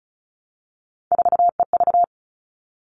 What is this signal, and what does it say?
Morse code, heard as a keyed tone, about 34 wpm, 700 Hz. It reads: 4EV
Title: All Cut Numbers - 3-Digit 35wpm